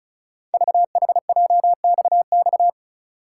Morse code, heard as a keyed tone, about 36 wpm, 690 Hz. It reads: VHJXX
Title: Code Groups - 5 Letters - 35wpm